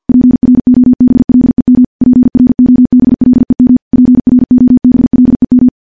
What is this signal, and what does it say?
Morse code, heard as a keyed tone, about 50 wpm, 250 Hz. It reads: PROBLEM PROBLEM PROBLEM
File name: morse / morse-training-50WPM